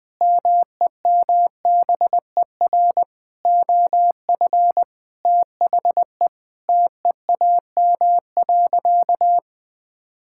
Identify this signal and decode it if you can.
Morse code, heard as a keyed tone, about 21 wpm, 700 Hz. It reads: MEMBER OF THE TEAM.
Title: Sentences with 5-Word Phrases - Spoken Beginning - 20wpm